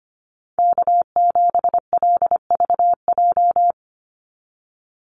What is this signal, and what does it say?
Morse code, heard as a keyed tone, about 25 wpm, 700 Hz. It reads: K7LVJ
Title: Call Signs Repeated 3x - 25wpm